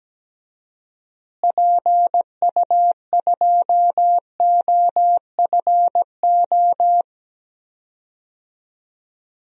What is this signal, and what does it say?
Morse code, heard as a keyed tone, about 17 wpm, 700 Hz. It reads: PU2OFO
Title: DXCC-100 Call Signs - 6 Characters - Repeated 3x - 17wpm